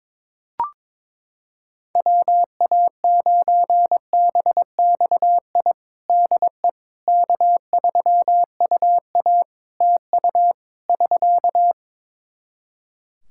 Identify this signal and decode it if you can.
Morse code, heard as a keyed tone, about 22 wpm, 700 Hz. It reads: WA9BXI DE K3UA TU <SK>
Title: K1USN SST QSO Practice 22wpm